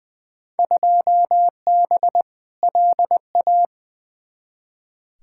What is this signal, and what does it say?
Morse code, heard as a keyed tone, about 20 wpm, 700 Hz. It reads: 2B LA
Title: Field Day Exchanges Encore - 20wpm